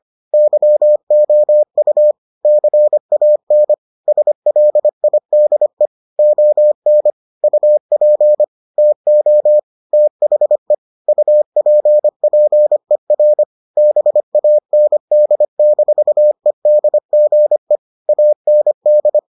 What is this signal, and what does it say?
Morse code, heard as a keyed tone, about 25 wpm, 600 Hz. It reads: YOU CAN SLIDE ON UP TO THE UPPER BAND-EDGE AND